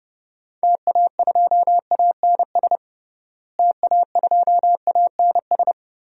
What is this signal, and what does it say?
Morse code, heard as a keyed tone, about 32 wpm, 700 Hz. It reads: TA2ANS TA2ANS
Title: DXCC-100 Call Signs - 6 Characters - Repeated 3x - 30wpm